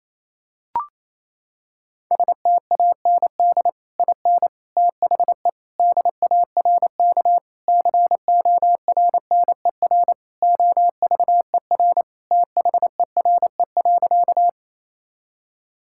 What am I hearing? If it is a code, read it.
Morse code, heard as a keyed tone, about 28 wpm, 700 Hz. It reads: STAND IN THE DARK CORNER OVER THERE.
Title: Sentences from Top 1600 Words 28wpm